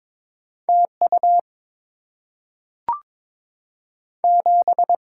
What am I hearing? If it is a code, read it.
Morse code, heard as a keyed tone, about 22 wpm, 700 Hz. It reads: T U 7
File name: practice / 22wpm / Two Letter-Number Combinations - 22wpm